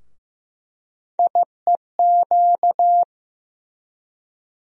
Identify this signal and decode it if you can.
Morse code, heard as a keyed tone, about 15 wpm, 700 Hz. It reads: IEQ
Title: USA Call Sign Postfixes - 3 Letters 15wpm